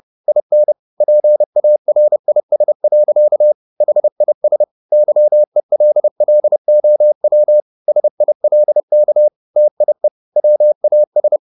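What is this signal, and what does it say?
Morse code, heard as a keyed tone, about 30 wpm, 600 Hz. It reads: IN PARIS. HIS YELLOW SILK TIE WAS
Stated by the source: Morse code at 30 wpm, the novel Never by Ken Follett in English, from CW med Gnister.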